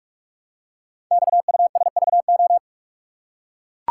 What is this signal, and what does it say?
Morse code, heard as a keyed tone, about 44 wpm, 700 Hz. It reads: XUSVO
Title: Code Groups - 5 Letter:Numbers - 45wpm